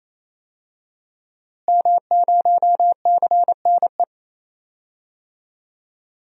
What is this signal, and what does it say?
Morse code, heard as a keyed tone, about 28 wpm, 700 Hz. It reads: M0CNE
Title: DXCC-100 Call Signs - 5 Characters - Repeated 3x - 28wpm